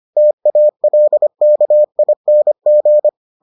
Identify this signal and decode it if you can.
Morse code, heard as a keyed tone, about 25 wpm, 600 Hz. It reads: TALKING